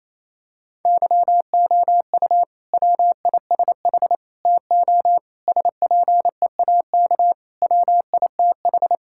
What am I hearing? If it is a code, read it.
Morse code, heard as a keyed tone, about 28 wpm, 700 Hz. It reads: YOU WISH TO SPEAK WITH
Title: Sentences from Top 1600 Words 28wpm